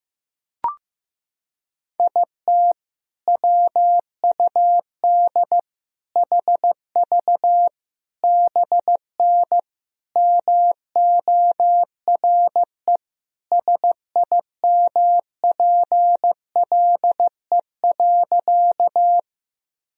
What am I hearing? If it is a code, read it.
Morse code, heard as a keyed tone, about 15 wpm, 700 Hz. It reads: IT WUD HV BN MORE SIMPLE.
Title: Sentences from Top 400 Words and Abbreviations 15wpm